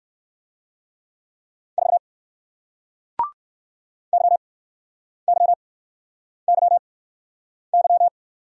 Morse code, heard as a keyed tone, about 56 wpm, spaced as for 9 wpm, 700 Hz. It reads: 4 YYYY
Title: Speed-Racing Single Letter-Number 45wpm